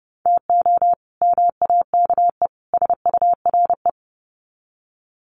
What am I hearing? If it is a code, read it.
Morse code, heard as a keyed tone, about 30 wpm, 700 Hz. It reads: TO MAKE SURE
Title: Sets of 3 Words 30wpm